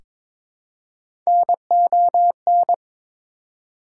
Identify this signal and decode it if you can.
Morse code, heard as a keyed tone, about 22 wpm, 700 Hz. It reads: NON
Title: Prefixes and Suffixes 22wpm